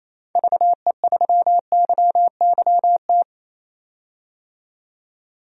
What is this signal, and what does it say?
Morse code, heard as a keyed tone, about 28 wpm, 700 Hz. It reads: VE3YYT
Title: DXCC-100 Call Signs - 6 Characters - 28wpm